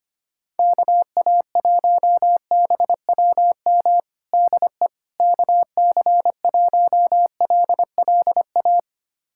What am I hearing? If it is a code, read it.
Morse code, heard as a keyed tone, about 25 wpm, 700 Hz. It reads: KA1BWM DE KC1LLA
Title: USA Callsign DE Callsign 25wpm